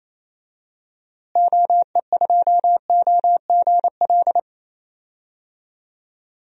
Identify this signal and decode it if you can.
Morse code, heard as a keyed tone, about 28 wpm, 700 Hz. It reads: OE2OGL
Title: DXCC-100 Call Signs - 6 Characters - Repeated 3x - 28wpm